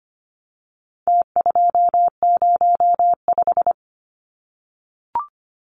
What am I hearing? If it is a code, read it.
Morse code, heard as a keyed tone, about 25 wpm, 700 Hz. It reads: T205
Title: Numbers - Contest Style - 4-Digit 25wpm